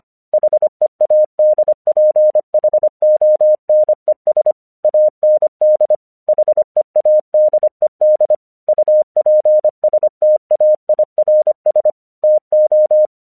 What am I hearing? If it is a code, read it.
Morse code, heard as a keyed tone, about 24 wpm, 610 Hz. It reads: HEADPHONES AND HEADED UPSTAIRS TO
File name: 1 English 25wpm / AQC28